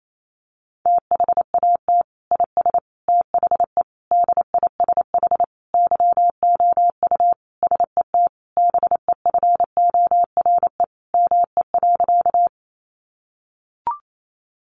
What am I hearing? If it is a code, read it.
Morse code, heard as a keyed tone, about 28 wpm, 700 Hz. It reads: THAT IS THE DISH YOU SET BEFORE ME.
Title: Sentences from Top 1300 Words 28wpm